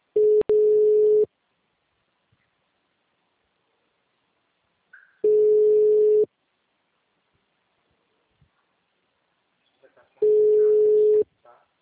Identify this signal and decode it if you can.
Ringback tone (Europe, 425 Hz), with speech.